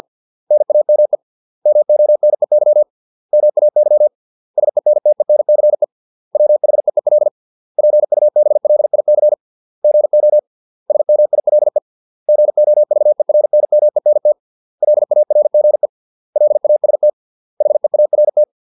Morse code, heard as a keyed tone, about 50 wpm, 600 Hz. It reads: NAME MONEY MAY SENTENCE WHEEL PUBLIC GO SMILE GOVERNMENT LARGE LAST HEART